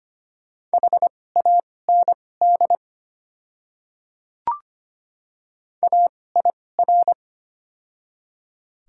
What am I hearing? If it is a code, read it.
Morse code, heard as a keyed tone, about 25 wpm, spaced as for 18 wpm, 690 Hz. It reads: HAND AIR
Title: 2x Character Spacing - Top 200 Words - 25wpm